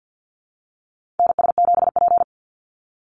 Morse code, heard as a keyed tone, about 50 wpm, 700 Hz. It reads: NS7P
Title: NA SOTA Activator Callsigns 50wpm